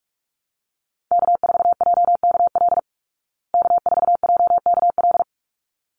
Morse code, heard as a keyed tone, about 44 wpm, 700 Hz. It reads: K4JKL K4JKL
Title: Call Signs Repeated 3x - 45wpm